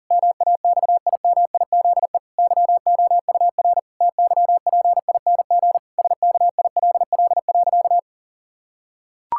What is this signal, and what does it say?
Morse code, heard as a keyed tone, about 40 wpm, 700 Hz. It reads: MAXIMIZE YOUR TYPING SKILL.
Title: Pangram Challenge v2 - 40wpm